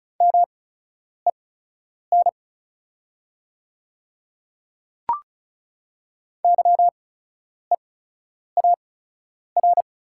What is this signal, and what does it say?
Morse code, heard as a keyed tone, about 36 wpm, spaced as for 10 wpm, 700 Hz. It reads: MEN YEAR
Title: 8x Character Spacing - Top 200 Words - 35wpm